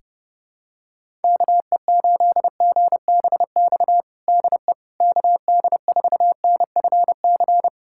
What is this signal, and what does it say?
Morse code, heard as a keyed tone, about 30 wpm, 700 Hz. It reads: KE8GBX DE KD4NFC
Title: USA Callsign DE Callsign - 6 Characters - 30wpm